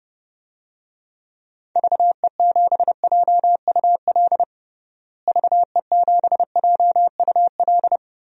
Morse code, heard as a keyed tone, about 32 wpm, 700 Hz. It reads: VE7JUL VE7JUL
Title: DXCC-100 Call Signs - 6 Characters - Repeated 3x - 30wpm